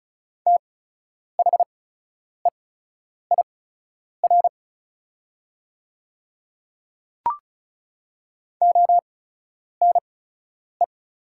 Morse code, heard as a keyed tone, about 36 wpm, spaced as for 10 wpm, 700 Hz. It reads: THEIR ONE
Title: 8x Character Spacing - Top 100 Words - 35wpm